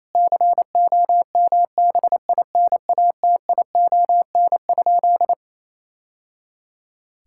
Morse code, heard as a keyed tone, about 28 wpm, 700 Hz. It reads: COMBINATION?
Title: Sentences from Top 1000 Words - Review Part 6 - 28wpm